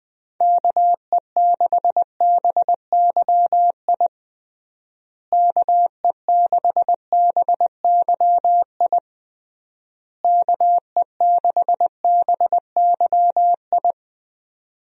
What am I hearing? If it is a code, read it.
Morse code, heard as a keyed tone, about 20 wpm, 700 Hz. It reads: KE6BYI KE6BYI KE6BYI
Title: USA Call Signs - 6 Characters - Repeated 3x - 20wpm